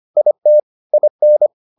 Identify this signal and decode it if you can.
Morse code, heard as a keyed tone, about 25 wpm, 600 Hz. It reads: IT IN